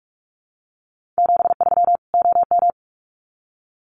Morse code, heard as a keyed tone, about 44 wpm, 700 Hz. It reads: 73 OM
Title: Common QSO Phrases 45wpm